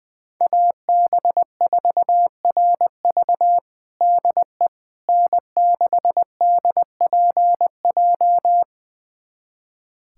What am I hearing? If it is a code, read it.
Morse code, heard as a keyed tone, about 20 wpm, 700 Hz. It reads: AB4RV DE N6DPJ
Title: USA Callsign DE Callsign - 5 Characters - 20wpm